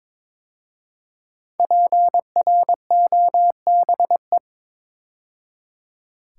Morse code, heard as a keyed tone, about 22 wpm, 700 Hz. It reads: PROBE
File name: 5 Letter Words - First 3 Letters Given - 22wpm